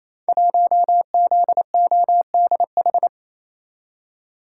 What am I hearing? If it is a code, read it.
Morse code, heard as a keyed tone, about 28 wpm, 700 Hz. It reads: 1ZODH
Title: Code Groups - 5 Letter:Numbers - 28wpm